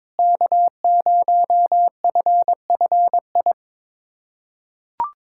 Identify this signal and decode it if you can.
Morse code, heard as a keyed tone, about 21 wpm, 700 Hz. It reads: K0FFI
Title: USA Callsign DE Callsign 22wpm